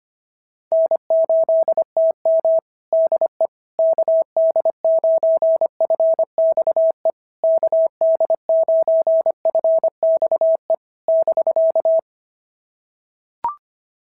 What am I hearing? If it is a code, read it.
Morse code, heard as a keyed tone, about 25 wpm, 650 Hz. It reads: N8TM DE KD9FXE KD9FXE <BK>
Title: SOTA QSO Practice 25wpm